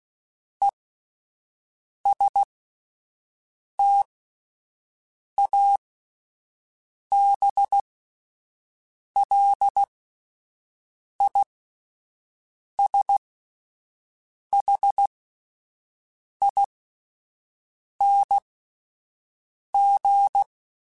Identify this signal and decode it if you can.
Morse code, heard as a keyed tone, about 16 wpm, spaced as for 5 wpm, 800 Hz. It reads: ESTABLISHING